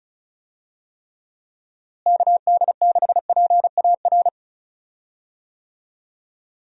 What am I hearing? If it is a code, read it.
Morse code, heard as a keyed tone, about 36 wpm, 700 Hz. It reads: KD6PAR
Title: USA Call Signs - 6 Characters - Repeated 3x - 35wpm